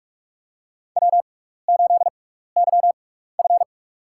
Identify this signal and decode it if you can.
Morse code, heard as a keyed tone, about 46 wpm, spaced as for 15 wpm, 700 Hz. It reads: W8YF
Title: USA Call Signs - 4 Characters - 6x Character Spacing - 45wpm